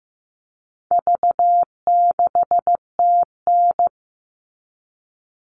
Morse code, heard as a keyed tone, about 15 wpm, 700 Hz. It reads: V6TN
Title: All Cut Numbers - 4-Digit 15wpm